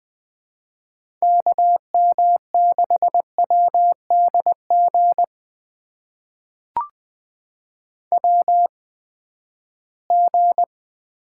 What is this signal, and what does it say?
Morse code, heard as a keyed tone, about 20 wpm, spaced as for 14 wpm, 700 Hz. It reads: KM6WDG WG
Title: USA Call Signs - 8x Character Spacing - 20wpm